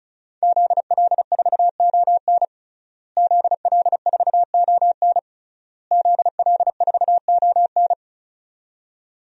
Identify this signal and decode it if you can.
Morse code, heard as a keyed tone, about 36 wpm, 700 Hz. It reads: ZL4ON ZL4ON ZL4ON
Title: DXCC-100 Call Signs - 5 Characters - Repeated 3x - 35wpm